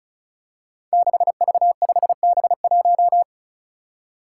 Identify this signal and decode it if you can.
Morse code, heard as a keyed tone, about 36 wpm, 700 Hz. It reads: 6V5B1
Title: Five Challenging Letter-Number Combinations 35wpm